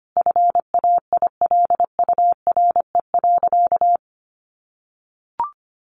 Morse code, heard as a keyed tone, about 25 wpm, 700 Hz. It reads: FAILURE.
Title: Sentences from Top 1300 Words 25wpm